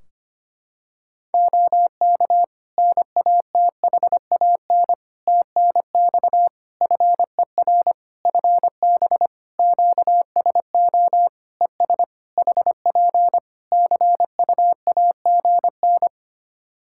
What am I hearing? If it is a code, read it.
Morse code, heard as a keyed tone, about 25 wpm, 700 Hz. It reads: OK NATHAN TNX FER FB QSO ES HP CUAGN
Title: Practice QSO - Exchange 4 - Ending - 25wpm